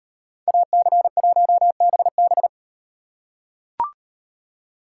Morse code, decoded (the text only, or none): AC1BB